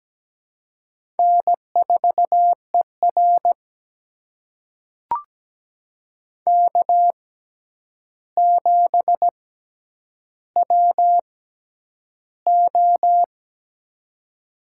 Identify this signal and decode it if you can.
Morse code, heard as a keyed tone, about 17 wpm, spaced as for 9 wpm, 700 Hz. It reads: N4ER K7WO
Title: USA Call Signs - 4 Characters - 6x Character Spacing - 17wpm